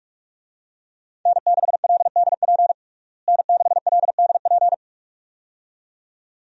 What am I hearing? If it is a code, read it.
Morse code, heard as a keyed tone, about 46 wpm, 700 Hz. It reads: N6LDP N6LDP
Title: USA Call Signs - 5 Characters - Repeated 2x - 45wpm